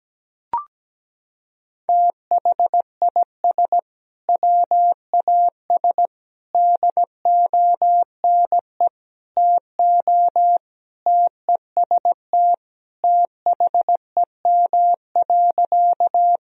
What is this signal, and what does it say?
Morse code, heard as a keyed tone, about 17 wpm, 700 Hz. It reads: THIS WAS DONE TO TEST THEM.
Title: Sentences from Top 1000 Words - Review Part 3 - 17wpm